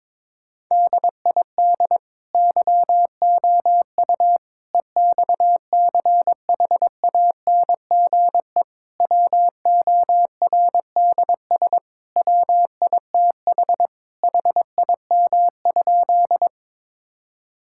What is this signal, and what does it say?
Morse code, heard as a keyed tone, about 21 wpm, 700 Hz. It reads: DID YOU EXCHANGE WORDS WITH HIM?
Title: Sentences from Top 1000 Words - Review Part 6 - 22wpm